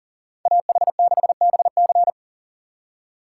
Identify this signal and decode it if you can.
Morse code, heard as a keyed tone, about 40 wpm, 700 Hz. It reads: AH6BC
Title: USA Call Signs - 5 Characters - 2x Character Spacing - 40wpm